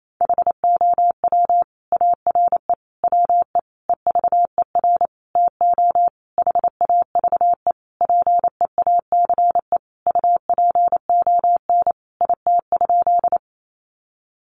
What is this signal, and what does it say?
Morse code, heard as a keyed tone, about 28 wpm, 700 Hz. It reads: HOW ARE WE EVER TO HAVE PEACE UPON IT?
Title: Sentences from Top 1500 Words 28wpm